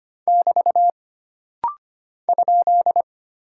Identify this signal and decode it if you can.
Morse code, heard as a keyed tone, about 25 wpm, 700 Hz. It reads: <BT> ?
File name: Single Punctuation-Prosign - Rapid-Fire - 25wpm